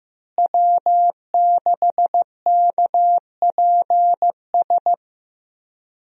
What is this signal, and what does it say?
Morse code, heard as a keyed tone, about 15 wpm, 700 Hz. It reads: W6KPS
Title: USA Call Signs - 5 Characters - Repeated 2x - 15wpm